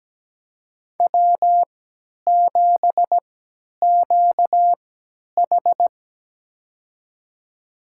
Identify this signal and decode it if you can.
Morse code, heard as a keyed tone, about 17 wpm, spaced as for 10 wpm, 700 Hz. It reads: W7QH
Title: USA Call Signs - 4 Characters - 3x Character Spacing - 17wpm